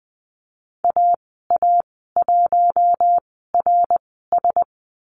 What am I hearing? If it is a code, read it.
Morse code, heard as a keyed tone, about 20 wpm, spaced as for 14 wpm, 700 Hz. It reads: AA1RS
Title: 082 Lesson 14g - Number 2 - Call Signs 20wpm